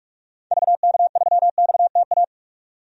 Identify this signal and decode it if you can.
Morse code, heard as a keyed tone, about 46 wpm, 700 Hz. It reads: VK3XTA
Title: DXCC-100 Call Signs - 6 Characters - 45wpm